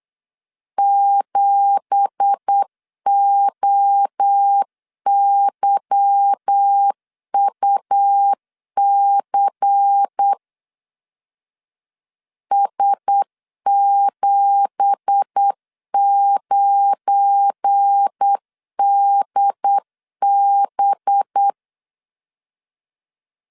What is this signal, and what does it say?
Morse code, heard as a keyed tone, about 8 wpm, 800 Hz. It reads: 7OYUC S79DB
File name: LEZIONI / 9 2867 8wpm 3440